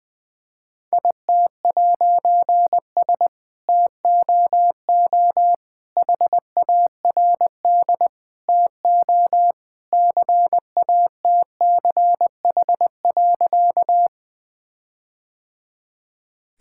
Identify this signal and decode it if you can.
Morse code, heard as a keyed tone, about 20 wpm, 700 Hz. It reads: IT'S TOO HARD TO CATCH.
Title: All-Age Appropriate Jokes - 20wpm